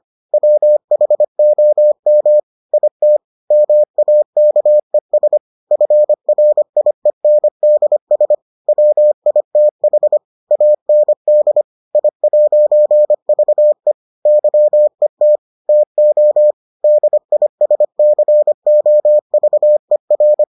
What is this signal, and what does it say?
Morse code, heard as a keyed tone, about 25 wpm, 600 Hz. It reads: WHOM IT MAKES FRIENDS WITH AND I'VE YET TO DISCOVER